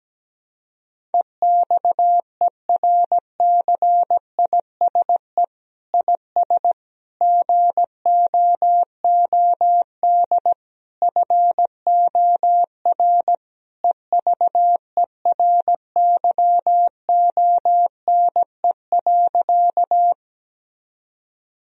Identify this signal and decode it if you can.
Morse code, heard as a keyed tone, about 17 wpm, 700 Hz. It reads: EXERCISE IS GOOD FOR EVERYONE.
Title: Sentences from Top 1000 Words - Review Part 4 - 17wpm